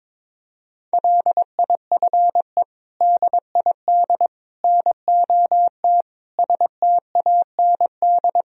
Morse code, heard as a keyed tone, about 22 wpm, 700 Hz. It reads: LIFE DID NOT STAND
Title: Sentences from Top 1000 Words - Review Part 1 - 22wpm